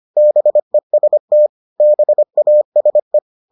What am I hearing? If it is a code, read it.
Morse code, heard as a keyed tone, about 25 wpm, 600 Hz. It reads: BEST BASE